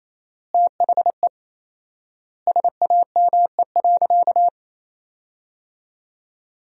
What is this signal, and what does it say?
Morse code, heard as a keyed tone, about 28 wpm, 700 Hz. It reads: THE SAME.
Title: 4x Word-Spacing - Sentences from Top 300 Words - 28wpm